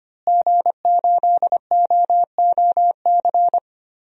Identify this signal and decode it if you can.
Morse code, heard as a keyed tone, about 25 wpm, 700 Hz. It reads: G8OOC
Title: DXCC-100 Call Signs - 5 Characters - Repeated 3x - 25wpm